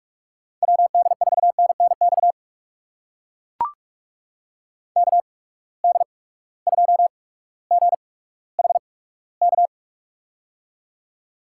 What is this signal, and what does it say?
Morse code, heard as a keyed tone, about 42 wpm, spaced as for 19 wpm, 700 Hz. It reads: WD4NNX KD2GHK
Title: USA Call Signs - 6 Characters - 8x Character Spacing - 45wpm